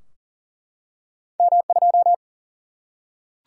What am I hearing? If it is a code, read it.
Morse code, heard as a keyed tone, about 40 wpm, 700 Hz. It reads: M2
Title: Single-Letter Single-Number 40wpm